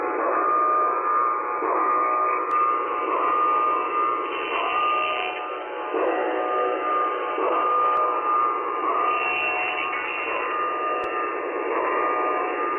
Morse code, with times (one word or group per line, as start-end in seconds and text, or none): none